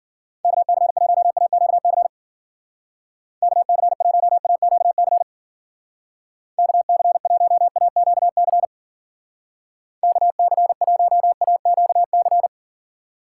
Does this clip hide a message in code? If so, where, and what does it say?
Yes: Morse code from 0.4 to 12.5 s, heard as a keyed tone, about 52 wpm, 700 Hz. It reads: KC1AQC KC1AQC KC1AQC KC1AQC